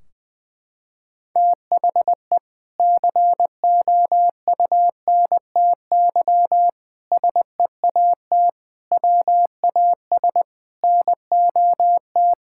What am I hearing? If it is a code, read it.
Morse code, heard as a keyed tone, about 20 wpm, 700 Hz. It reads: THE COUNTY SEAT WAS NOT
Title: Sentences from Top 1600 Words 20wpm